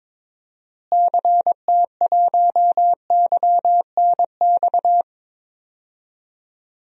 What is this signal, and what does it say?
Morse code, heard as a keyed tone, about 22 wpm, 700 Hz. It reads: CT1YNX
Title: DXCC-100 Call Signs - 6 Characters - Repeated 3x - 22wpm